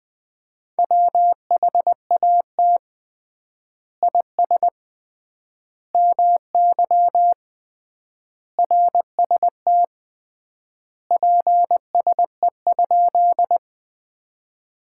Morse code, heard as a keyed tone, about 20 wpm, 700 Hz. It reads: WHAT IS MY RST PSE?
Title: Common QSO Phrases - 3x Word-Spacing - 20wpm